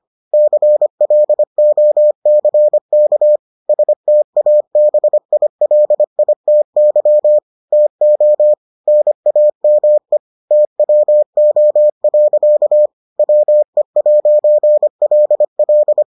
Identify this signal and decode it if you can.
Morse code, heard as a keyed tone, about 25 wpm, 600 Hz. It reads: CLOCK STABILITY TO NAME TWO. WE'LL